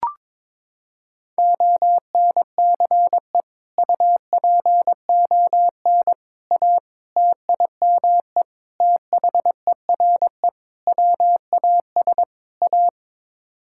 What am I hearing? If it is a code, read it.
Morse code, heard as a keyed tone, about 21 wpm, 700 Hz. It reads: ONCE UPON A TIME THERE WAS A
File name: practice / 22wpm / Sets of 7 Words - 22wpm